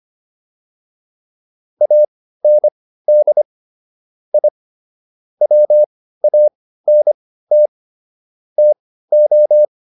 Morse code, heard as a keyed tone, about 25 wpm, spaced as for 15 wpm, 600 Hz. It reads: AND I WANT TO